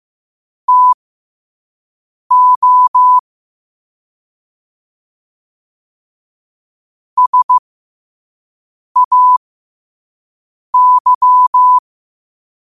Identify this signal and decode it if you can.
Morse code, heard as a keyed tone, about 15 wpm, spaced as for 5 wpm, 1000 Hz. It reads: TO SAY
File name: qotdfiles / 5wpm